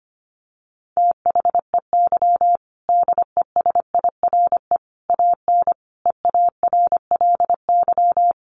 Morse code, heard as a keyed tone, about 25 wpm, 700 Hz. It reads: THEY DESIRE AN EARLY
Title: Sentences from Top 1100 Words 25wpm